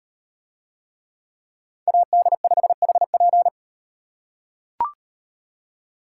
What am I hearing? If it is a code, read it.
Morse code, heard as a keyed tone, about 38 wpm, 700 Hz. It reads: AD5HP